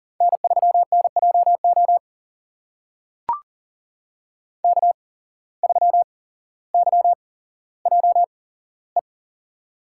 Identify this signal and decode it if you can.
Morse code, heard as a keyed tone, about 40 wpm, spaced as for 17 wpm, 700 Hz. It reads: N3NJO K3YJE
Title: USA Call Signs - 5 Characters - 8x Character Spacing - 40wpm